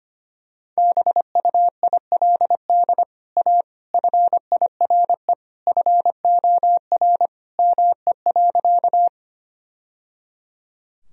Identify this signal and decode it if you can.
Morse code, heard as a keyed tone, about 25 wpm, 700 Hz. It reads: BUILD A FIRE FOR ME.
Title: Sentences from Top 1000 Words - Review Part 1 - 25wpm